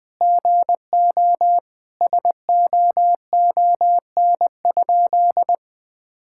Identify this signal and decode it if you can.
Morse code, heard as a keyed tone, about 20 wpm, 700 Hz. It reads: GO SOON?
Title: Sentences with 3-Word Phrases - 20wpm